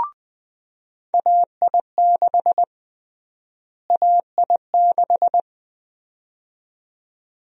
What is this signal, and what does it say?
Morse code, heard as a keyed tone, about 20 wpm, 700 Hz. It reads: AI6 AI6
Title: USA Call Sign Prefixes - Repeated 2x - 20wpm